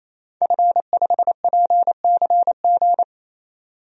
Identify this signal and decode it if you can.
Morse code, heard as a keyed tone, about 28 wpm, 700 Hz. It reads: F5PCG
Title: DXCC-100 Call Signs - 5 Characters - 28wpm